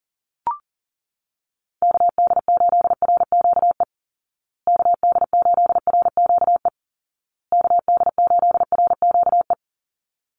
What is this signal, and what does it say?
Morse code, heard as a keyed tone, about 40 wpm, 700 Hz. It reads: KD8RQE KD8RQE KD8RQE
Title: Call Signs Repeated 2x - 40wpm